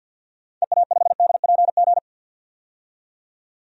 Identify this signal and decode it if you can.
Morse code, heard as a keyed tone, about 51 wpm, 700 Hz. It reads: EA5DPG
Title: DXCC-100 Call Signs - 6 Characters - 50wpm